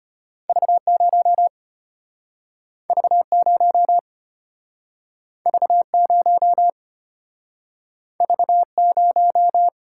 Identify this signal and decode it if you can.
Morse code, heard as a keyed tone, about 34 wpm, 700 Hz. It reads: V0 V0 V0 V0